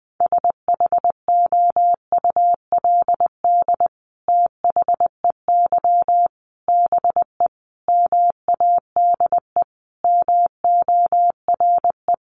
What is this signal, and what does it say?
Morse code, heard as a keyed tone, about 20 wpm, 700 Hz. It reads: SHOULD THEY BE MADE MORE
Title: Sentences from Top 1400 Words 20wpm